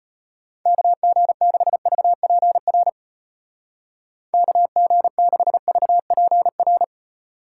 Morse code, heard as a keyed tone, about 32 wpm, 700 Hz. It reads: KG6VPR KG6VPR